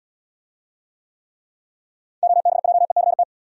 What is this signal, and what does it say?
Morse code, heard as a keyed tone, about 73 wpm, 700 Hz. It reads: OBJECT